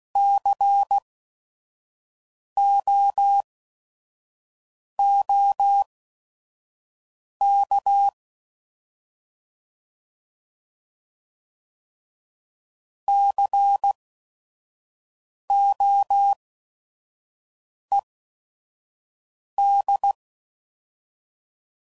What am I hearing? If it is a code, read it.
Morse code, heard as a keyed tone, about 16 wpm, spaced as for 5 wpm, 800 Hz. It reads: COOK COED
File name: Prefixes / By First Letter / Cs